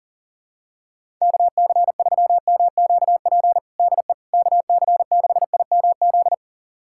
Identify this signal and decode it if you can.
Morse code, heard as a keyed tone, about 40 wpm, 700 Hz. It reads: KC3MQP DE KC6IMZ
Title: USA Callsign DE Callsign 40wpm